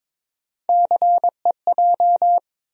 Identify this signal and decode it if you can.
Morse code, heard as a keyed tone, about 22 wpm, 700 Hz. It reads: CEJ
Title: USA Call Sign Postfixes - 3 Letters 22wpm